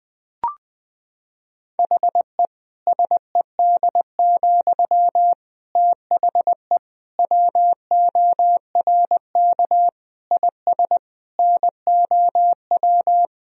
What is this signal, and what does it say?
Morse code, heard as a keyed tone, about 21 wpm, 700 Hz. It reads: HE SED, THE WORK IS NOW
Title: Sentences from Top 400 Words and Abbreviations 20wpm